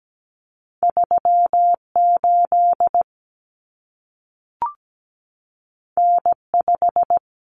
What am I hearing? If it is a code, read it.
Morse code, heard as a keyed tone, about 17 wpm, 700 Hz. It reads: 38 N5
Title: Common Cut Numbers - 2-Digit 17wpm